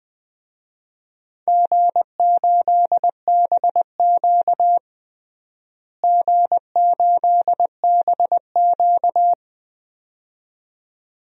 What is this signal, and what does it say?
Morse code, heard as a keyed tone, about 20 wpm, 700 Hz. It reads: G8BQ G8BQ
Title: DXCC-100 Call Signs - 4 Characters - Repeated 2x - 20wpm